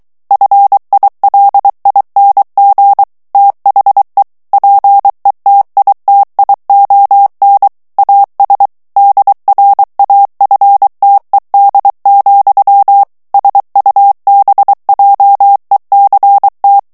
Morse code, heard as a keyed tone, about 23 wpm, 800 Hz. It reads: FILING THE PETITION AS DRAFTED, SUBJECT
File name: ANGLAIS texte clair 3 vitesse 22 mots minute